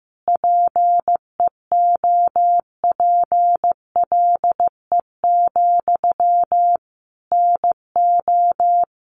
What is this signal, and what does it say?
Morse code, heard as a keyed tone, about 15 wpm, 700 Hz. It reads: PEOPLE, NO
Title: Sentences from Top 1500 Words 15wpm